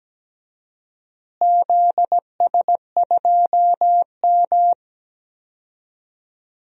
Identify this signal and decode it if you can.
Morse code, heard as a keyed tone, about 17 wpm, 700 Hz. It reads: ZS2M
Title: DXCC-100 Call Signs - 4 Characters - Repeated 3x - 17wpm